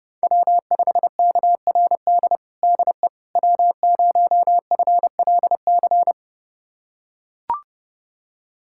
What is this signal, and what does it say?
Morse code, heard as a keyed tone, about 30 wpm, 700 Hz. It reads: W5KRD DE W0FLC
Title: USA Callsign DE Callsign - 5 Characters - 30wpm